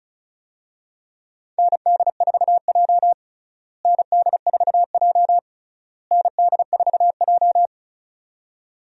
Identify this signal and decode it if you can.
Morse code, heard as a keyed tone, about 36 wpm, 700 Hz. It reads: ND4J ND4J ND4J
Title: USA Call Signs - 4 Characters - Repeated 3x - 35wpm